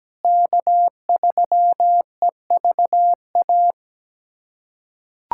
Morse code, heard as a keyed tone, about 17 wpm, 700 Hz. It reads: K3EVA
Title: USA Callsign DE Callsign - 5 Characters - 17wpm